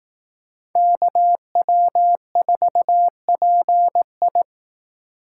Morse code, heard as a keyed tone, about 18 wpm, 700 Hz. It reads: KW4PI
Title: Speed Racing - USA Call Signs 15wpm